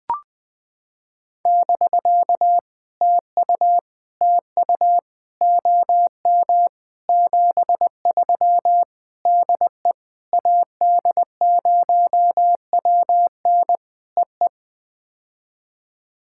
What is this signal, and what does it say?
Morse code, heard as a keyed tone, about 21 wpm, 700 Hz. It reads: <BK> TU TU OM 73 DE AD0WN EE
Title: POTA QSO Practice - 20wpm